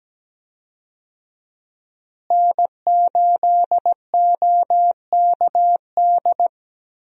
Morse code, heard as a keyed tone, about 17 wpm, 700 Hz. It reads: N8OKD
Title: USA Call Signs - 5 Characters - Repeated 2x - 17wpm